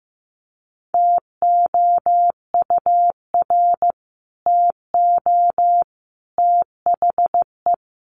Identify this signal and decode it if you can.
Morse code, heard as a keyed tone, about 15 wpm, 700 Hz. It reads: TOUR TO THE
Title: Sentences from Top 1400 Words 15wpm